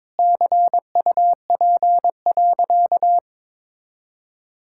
Morse code, heard as a keyed tone, about 22 wpm, 700 Hz. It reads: CUP.
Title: Sentences from Top 1600 Words 22wpm